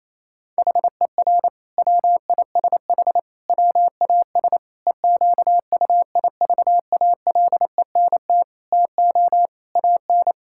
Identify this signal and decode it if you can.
Morse code, heard as a keyed tone, about 28 wpm, 700 Hz. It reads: HER WISH WAS EQUIVALENT TO AN
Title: Sentences from Top 1600 Words 28wpm